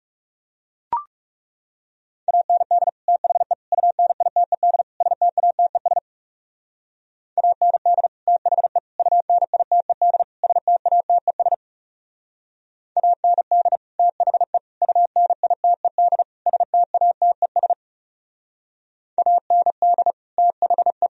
Morse code, heard as a keyed tone, about 40 wpm, 700 Hz. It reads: AND THE UNITED STATES AND THE UNITED STATES AND THE UNITED STATES AND THE